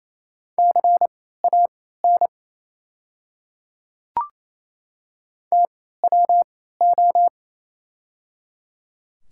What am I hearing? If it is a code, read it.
Morse code, heard as a keyed tone, about 28 wpm, spaced as for 16 wpm, 690 Hz. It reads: CAN TWO
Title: 3x Character Spacing - Top 100 Words - 28wpm